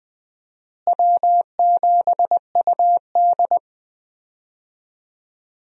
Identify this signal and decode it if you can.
Morse code, heard as a keyed tone, about 21 wpm, 700 Hz. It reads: W7UD
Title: USA Call Signs - 4 Characters - Repeated 2x - 20wpm